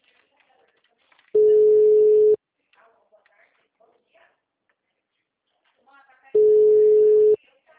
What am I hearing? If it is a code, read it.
Ringback tone (Europe, 425 Hz), with speech.
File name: out- -2036- - - .22502